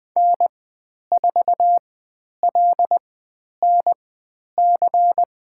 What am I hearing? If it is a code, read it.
Morse code, heard as a keyed tone, about 20 wpm, spaced as for 10 wpm, 700 Hz. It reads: N4LNC
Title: 109 Lesson 18g - Letter m - Call Signs 20wpm